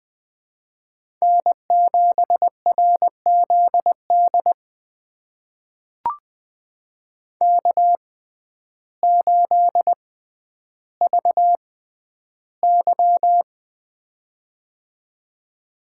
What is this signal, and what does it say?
Morse code, heard as a keyed tone, about 20 wpm, spaced as for 11 wpm, 700 Hz. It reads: N7RZD K8VY
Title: USA Call Signs - 6x Character Spacing - 20wpm